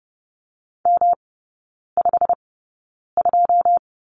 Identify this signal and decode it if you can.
Morse code, heard as a keyed tone, about 30 wpm, spaced as for 9 wpm, 700 Hz. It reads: M52
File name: Lesson 18e - Letter m - Three Characters 30wpm